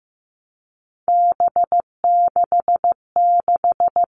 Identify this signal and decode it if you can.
Morse code, heard as a keyed tone, about 15 wpm, 700 Hz. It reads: B66
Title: All Cut Numbers - 3-Digit 15wpm